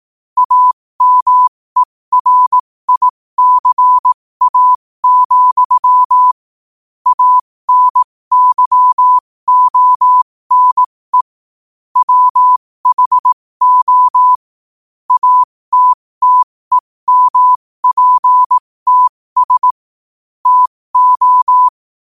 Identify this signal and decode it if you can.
Morse code, heard as a keyed tone, about 18 wpm, spaced as for 15 wpm, 1000 Hz. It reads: AMERICA, ANYONE WHO ATTEMPTS TO
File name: qotdfiles / 15wpm